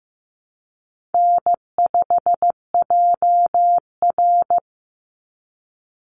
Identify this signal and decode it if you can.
Morse code, heard as a keyed tone, about 15 wpm, 700 Hz. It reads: N5JR
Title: Call Signs Repeated 3x - 15wpm